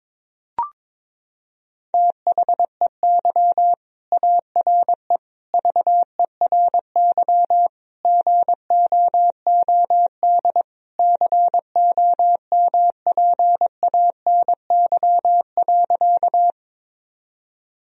Morse code, heard as a keyed tone, about 22 wpm, 700 Hz. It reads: THEY ARE VERY GOOD COMPANY.
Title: Sentences from Top 1000 Words - Review Part 3 - 22wpm